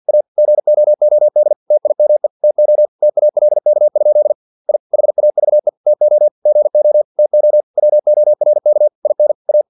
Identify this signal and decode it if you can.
Morse code, heard as a keyed tone, about 50 wpm, 600 Hz. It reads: A GOOD TIME TO TALK? I HAVE TO GO TO WORK IN A
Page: Never 49wpm